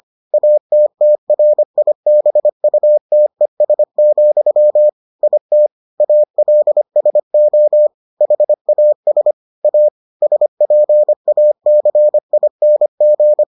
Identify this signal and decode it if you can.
Morse code, heard as a keyed tone, about 25 wpm, 600 Hz. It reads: ATTRIBUTES, IT ALSO HAS A SPACING